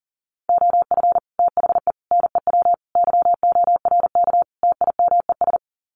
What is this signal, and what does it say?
Morse code, heard as a keyed tone, about 40 wpm, 700 Hz. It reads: OF THE NEW YORK TIMES
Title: Sets of 5 Words 40wpm